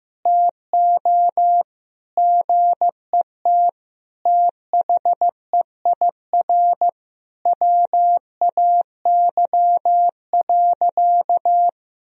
Morse code, heard as a keyed tone, about 15 wpm, 700 Hz. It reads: TO GET THEIR WAY.
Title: Sentences from Top 1000 Words - Review Part 4 - 15wpm